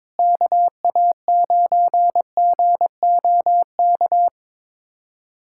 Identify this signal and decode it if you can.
Morse code, heard as a keyed tone, about 22 wpm, 700 Hz. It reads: KA9GOK
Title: USA Callsign DE Callsign - 6 Characters - 22wpm